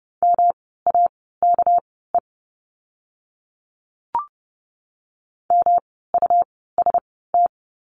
Morse code, heard as a keyed tone, about 30 wpm, spaced as for 17 wpm, 700 Hz. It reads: MAKE MUST
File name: course / 30:10wpm / 237 Lesson 38f - Letter q - Words 30wpm